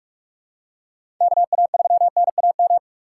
Morse code, heard as a keyed tone, about 42 wpm, 700 Hz. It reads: KA3NAM
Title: K1USN SST QSO Practice - 45wpm